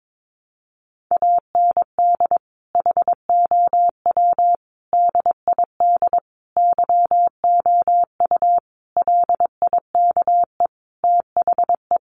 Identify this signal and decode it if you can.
Morse code, heard as a keyed tone, about 22 wpm, 700 Hz. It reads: AND HOW DID YOU LIKE THE
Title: Sentences from Top 1200 Words 22wpm